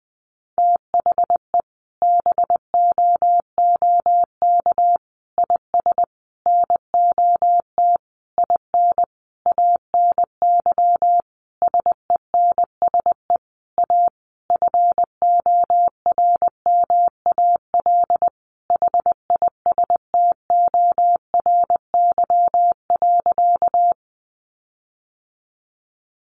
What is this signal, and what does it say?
Morse code, heard as a keyed tone, about 20 wpm, 700 Hz. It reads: THE BOOK IS NOT IN ANY SENSE A FORMAL HISTORY.
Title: Sentences from Top 1300 Words 20wpm